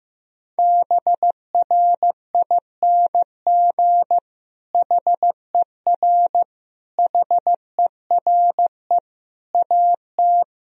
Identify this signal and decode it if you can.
Morse code, heard as a keyed tone, about 15 wpm, 700 Hz. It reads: BRING HER HERE AT
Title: Sentences from Top 1000 Words - Review Part 3 - 15wpm